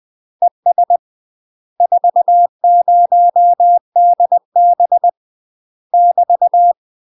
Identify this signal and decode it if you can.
Morse code, heard as a keyed tone, about 20 wpm, 700 Hz. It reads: ES 40DB <BT>
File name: UK MASTER QSOs TRAINING / 20WPM / UK MASTER QSOs 0031